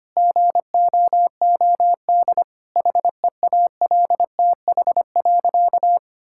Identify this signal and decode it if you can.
Morse code, heard as a keyed tone, about 25 wpm, 700 Hz. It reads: GOOD HEALTH.
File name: Easy Sentences with Hints - 25wpm